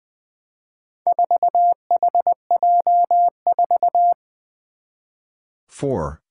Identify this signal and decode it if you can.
Morse code, heard as a keyed tone, about 20 wpm, 700 Hz. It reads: 4HJ4
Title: Four Challenging Letter-Number Combinations 20wpm